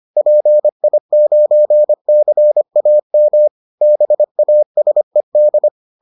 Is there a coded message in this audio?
Yes: Morse code from 0.2 to 5.7 s, heard as a keyed tone, about 25 wpm, 600 Hz. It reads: PI9CAM BASED